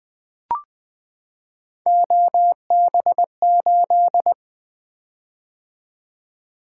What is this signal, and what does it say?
Morse code, heard as a keyed tone, about 20 wpm, 700 Hz. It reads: OB8
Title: Three Letter-Number Combinations - 20wpm